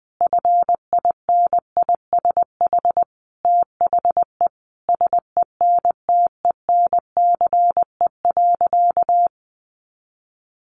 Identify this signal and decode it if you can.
Morse code, heard as a keyed tone, about 20 wpm, 700 Hz. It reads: FINISH THE SENTENCE.
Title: Sentences from Top 1200 Words 20wpm